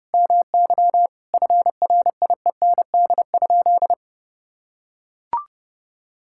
Morse code, heard as a keyed tone, about 30 wpm, 700 Hz. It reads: MY FRIEND?
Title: Sentences from Top 1000 Words - Review Part 5 - 30wpm